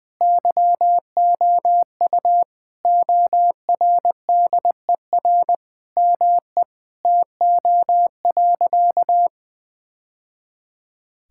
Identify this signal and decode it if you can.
Morse code, heard as a keyed tone, about 20 wpm, 700 Hz. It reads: YOU ORDER ME TO.
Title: Sentences from Top 1000 Words - Review Part 6 - 20wpm